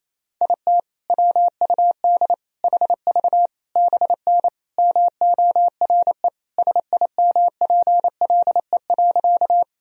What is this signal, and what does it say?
Morse code, heard as a keyed tone, about 28 wpm, 700 Hz. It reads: IT WUD HV BN MORE SIMPLE.
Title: Sentences from Top 400 Words and Abbreviations 28wpm